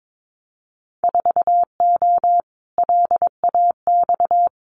Morse code, heard as a keyed tone, about 22 wpm, 700 Hz. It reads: 4O LAX
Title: Winter Field Day Exchanges Encore 22wpm